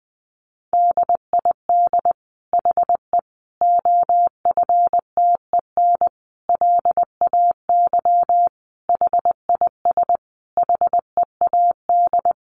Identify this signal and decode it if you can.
Morse code, heard as a keyed tone, about 20 wpm, 700 Hz. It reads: DID HE OFTEN LAY HIS HEAD
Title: Sentences from Top 1000 Words 20wpm